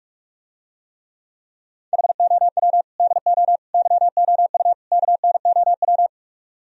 Morse code, heard as a keyed tone, about 48 wpm, 700 Hz. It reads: HOW DO YOU KNOW